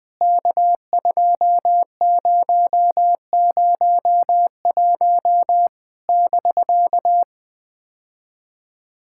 Morse code, heard as a keyed tone, about 20 wpm, 700 Hz. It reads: K2001 <BK>
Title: POTA QSO Practice - 20wpm